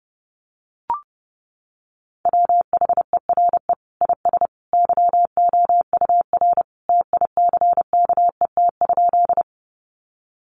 Morse code, heard as a keyed tone, about 30 wpm, 700 Hz. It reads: WHERE IS YOUR TICKET?
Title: Sentences from Top 1400 Words 30wpm